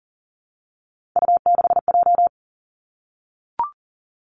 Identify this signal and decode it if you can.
Morse code, heard as a keyed tone, about 42 wpm, 700 Hz. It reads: U6J
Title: Three Challenging Letter-Number Combinations 40wpm